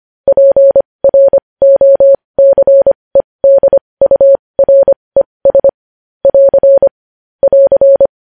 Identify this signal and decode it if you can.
Morse code, heard as a keyed tone, about 25 wpm, spaced as for 20 wpm, 550 Hz. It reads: PROCEDURES <AR> <AR>
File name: podcast / News headlines-1x-25x20